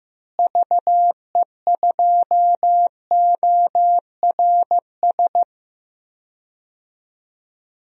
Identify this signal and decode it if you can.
Morse code, heard as a keyed tone, about 15 wpm, 700 Hz. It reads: VE2ORS
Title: DXCC-100 Call Signs - 6 Characters - Repeated 3x - 15wpm